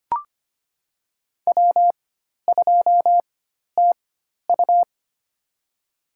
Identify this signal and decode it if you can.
Morse code, heard as a keyed tone, about 25 wpm, spaced as for 12 wpm, 700 Hz. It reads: W2TU
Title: USA Call Signs - 4x Character Spacing - 25wpm